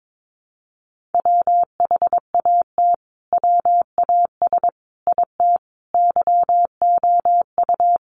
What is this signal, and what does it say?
Morse code, heard as a keyed tone, about 22 wpm, 700 Hz. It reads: WHAT WAS IT YOU
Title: Sentences from Top 200 Words 22wpm